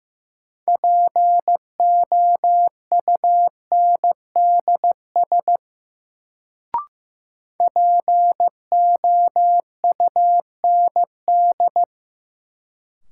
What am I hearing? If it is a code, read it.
Morse code, heard as a keyed tone, about 15 wpm, 700 Hz. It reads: POUNDS POUND
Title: Sentences from Top 1000 Words - Review Part 2 - New Word Drill - 15wpm